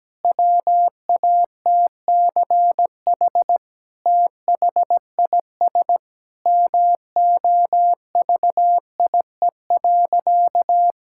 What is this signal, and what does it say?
Morse code, heard as a keyed tone, about 17 wpm, 700 Hz. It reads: WATCH THIS MOVIE.
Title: Sentences from Top 1000 Words - Review Part 5 - 17wpm